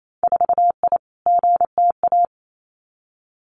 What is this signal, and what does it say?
Morse code, heard as a keyed tone, about 28 wpm, 700 Hz. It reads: 4I GTA
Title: Winter Field Day Exchanges Encore 28wpm